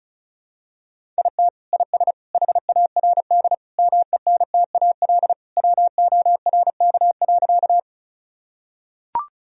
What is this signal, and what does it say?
Morse code, heard as a keyed tone, about 36 wpm, 700 Hz. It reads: IT IS HARD MENTAL WORK.
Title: Sentences from Top 1000 Words - Review Part 4 - 35wpm